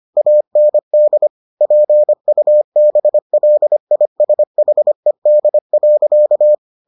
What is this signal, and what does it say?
Morse code, heard as a keyed tone, about 25 wpm, 600 Hz. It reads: AND PUBLISHED.